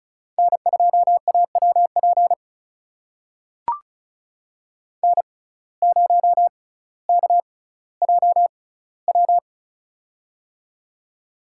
Morse code, heard as a keyed tone, about 36 wpm, spaced as for 18 wpm, 700 Hz. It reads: N2AWP N0KJW
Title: USA Call Signs - 5 Characters - 6x Character Spacing - 35wpm